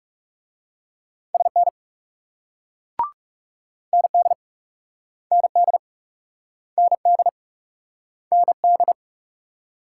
Morse code, heard as a keyed tone, about 41 wpm, 700 Hz. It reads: SN ND ND ND ND